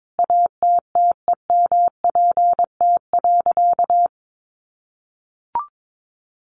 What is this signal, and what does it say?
Morse code, heard as a keyed tone, about 22 wpm, 700 Hz. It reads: ATTEMPT.
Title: Sentences from Top 1000 Words 22wpm